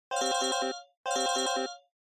There is no code.